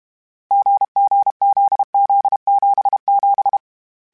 Morse code, heard as a keyed tone, about 32 wpm, 800 Hz. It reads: GGZZ77
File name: Difficult Combos / Difficult Combos-mp3s / GZ GZ7 @30